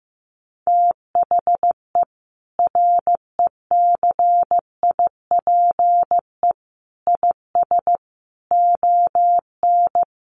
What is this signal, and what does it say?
Morse code, heard as a keyed tone, about 14 wpm, 700 Hz. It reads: THE RECIPE IS ON
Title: Sentences from Top 1200 Words 15wpm